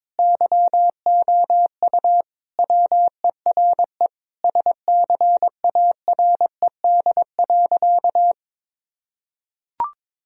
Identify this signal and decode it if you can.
Morse code, heard as a keyed tone, about 22 wpm, 700 Hz. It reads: YOU WERE SCARED.
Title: Sentences from Top 1000 Words - Review Part 4 - 22wpm